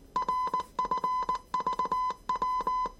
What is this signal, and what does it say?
Morse code, heard as a keyed tone, about 19 wpm, 1010 Hz. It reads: RFVW
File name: 业余无线电 / ●未分类 / 字码80